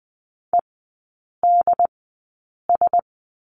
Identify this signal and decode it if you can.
Morse code, heard as a keyed tone, about 20 wpm, spaced as for 8 wpm, 700 Hz. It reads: EDS
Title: 093 Lesson 16e - Letter c - Three Characters 20wpm